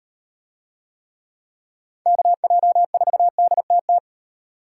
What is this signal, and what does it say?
Morse code, heard as a keyed tone, about 34 wpm, 700 Hz. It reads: KJ4DTT